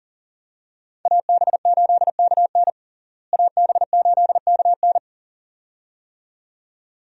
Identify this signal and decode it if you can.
Morse code, heard as a keyed tone, about 40 wpm, 700 Hz. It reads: AB8KN AB8KN
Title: USA Call Signs - 5 Characters - Repeated 2x - 40wpm